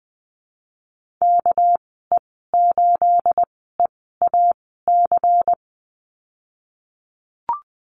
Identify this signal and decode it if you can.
Morse code, heard as a keyed tone, about 20 wpm, spaced as for 15 wpm, 700 Hz. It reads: KE8EAC
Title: 231 Lesson 37g - Letter x - Call Signs 20wpm